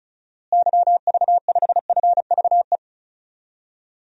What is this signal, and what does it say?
Morse code, heard as a keyed tone, about 36 wpm, 700 Hz. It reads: YV5FVE
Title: DXCC-100 Call Signs - 6 Characters - Repeated 3x - 35wpm